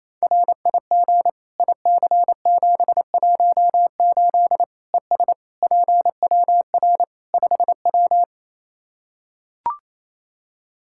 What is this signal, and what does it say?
Morse code, heard as a keyed tone, about 28 wpm, 700 Hz. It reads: RIG IC718 ES PWR 5W
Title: Practice QSO - Exchange 2 - Info - 28wpm